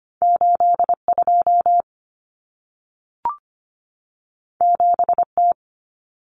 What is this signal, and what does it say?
Morse code, heard as a keyed tone, about 25 wpm, 700 Hz. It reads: 82 7T
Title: Common Cut Numbers - 2-Digit 25wpm